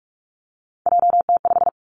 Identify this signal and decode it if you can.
Morse code, heard as a keyed tone, about 46 wpm, 700 Hz. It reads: JT5
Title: Three Challenging Letter-Number Combinations 45wpm